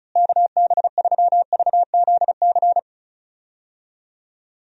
Morse code, heard as a keyed tone, about 36 wpm, 700 Hz. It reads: KB3VZC
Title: USA Call Signs - 6 Characters - Repeated 3x - 35wpm